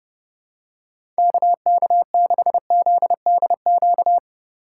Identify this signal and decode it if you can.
Morse code, heard as a keyed tone, about 31 wpm, 700 Hz. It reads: KK6ZDQ
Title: USA Call Signs - 6 Characters - 3x Character Spacing - 30wpm